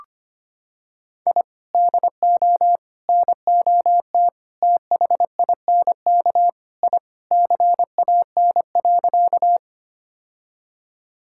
Morse code, heard as a keyed tone, about 25 wpm, 700 Hz. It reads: I DO NOT THINK I CAN.
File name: practice / 25wpm / Sentences with 4-Word Phrases - 25wpm